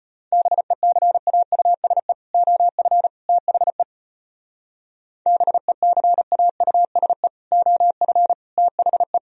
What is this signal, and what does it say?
Morse code, heard as a keyed tone, about 31 wpm, 700 Hz. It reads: BECAUSE OF THE BECAUSE OF THE